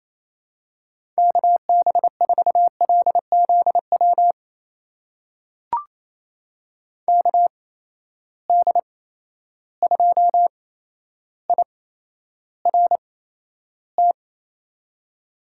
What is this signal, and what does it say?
Morse code, heard as a keyed tone, about 27 wpm, spaced as for 12 wpm, 700 Hz. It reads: KB4LZW KD2IRT
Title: USA Call Signs - 6 Characters - 8x Character Spacing - 28wpm